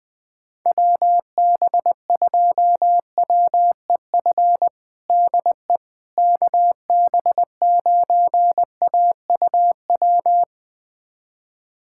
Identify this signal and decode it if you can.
Morse code, heard as a keyed tone, about 20 wpm, 700 Hz. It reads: WB2WEF DE KB9AUW
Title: USA Callsign DE Callsign 20wpm